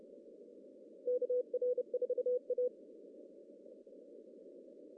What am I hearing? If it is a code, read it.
Morse code, heard as a keyed tone, about 30 wpm, 500 Hz. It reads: KR4A
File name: CW / cqww-cw-k5tr